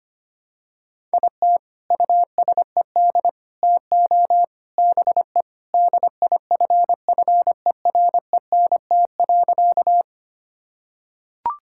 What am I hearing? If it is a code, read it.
Morse code, heard as a keyed tone, about 25 wpm, 700 Hz. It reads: IT USED TO BE DIFFERENT.
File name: practice / 25wpm / Sentences with 4-Word Phrases - 25wpm